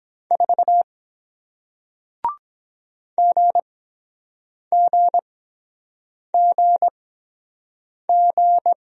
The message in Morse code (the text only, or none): 4 GGGG